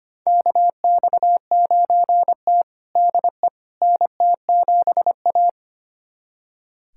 Morse code, heard as a keyed tone, about 25 wpm, 700 Hz. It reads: KX9T DE NT7A
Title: USA Callsign DE Callsign - 4 Characters - 25wpm